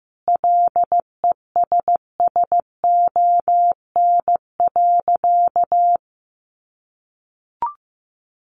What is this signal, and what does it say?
Morse code, heard as a keyed tone, about 15 wpm, 700 Hz. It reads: LESSON.
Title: Sentences from Top 1200 Words 15wpm